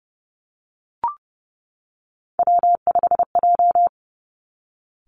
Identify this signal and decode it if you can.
Morse code, heard as a keyed tone, about 31 wpm, 700 Hz. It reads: W5J
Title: Three Challenging Letter-Number Combinations 30wpm